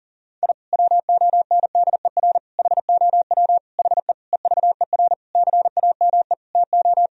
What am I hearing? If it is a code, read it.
Morse code, heard as a keyed tone, about 40 wpm, 700 Hz. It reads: I WONDER HOW HE EVER CAME TO
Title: Sentences from Top 1000 Words - Review Part 3 - 40wpm